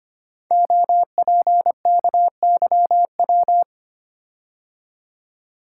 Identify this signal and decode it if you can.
Morse code, heard as a keyed tone, about 25 wpm, 700 Hz. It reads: OPKYW
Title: Code Groups - 5 Letter:Numbers - 25wpm